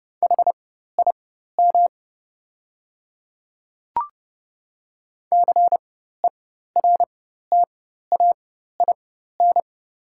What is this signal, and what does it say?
Morse code, heard as a keyed tone, about 30 wpm, spaced as for 14 wpm, 690 Hz. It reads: HIM CERTAIN
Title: 4x Character Spacing - Top 500 Words - 30wpm